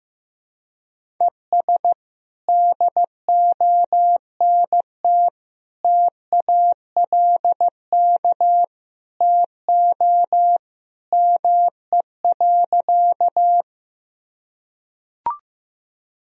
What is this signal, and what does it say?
Morse code, heard as a keyed tone, about 16 wpm, 700 Hz. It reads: ES DONT TALK TO ME.
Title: Sentences from Top 400 Words and Abbreviations 15wpm